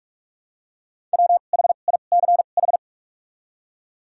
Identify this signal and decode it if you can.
Morse code, heard as a keyed tone, about 48 wpm, spaced as for 33 wpm, 700 Hz. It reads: WHICH
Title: 2x Character Spacing - Top 100 Words - 45wpm